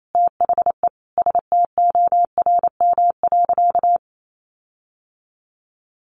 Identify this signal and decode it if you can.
Morse code, heard as a keyed tone, about 28 wpm, 700 Hz. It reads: THE STORM.
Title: Sentences from Top 1500 Words 28wpm